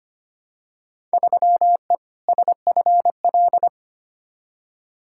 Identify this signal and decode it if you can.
Morse code, heard as a keyed tone, about 25 wpm, 700 Hz. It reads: 3E SFL
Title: Field Day Exchanges Encore - 25wpm